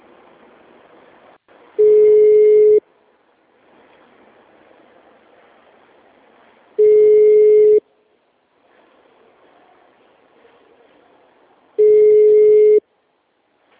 Ringback tone (Europe, 425 Hz).
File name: out- -2016- - - .39